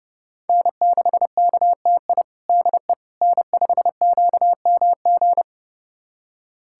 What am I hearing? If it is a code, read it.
Morse code, heard as a keyed tone, about 30 wpm, 700 Hz. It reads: N6KTI DE N5QMG
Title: USA Callsign DE Callsign - 5 Characters - 30wpm